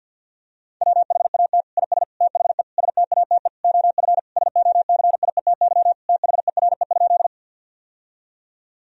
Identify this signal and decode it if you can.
Morse code, heard as a keyed tone, about 50 wpm, 700 Hz. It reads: WHAT IS THE STATE OF SOCIETY THERE?
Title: Sentences from Top 1000 Words - Review Part 4 - 50wpm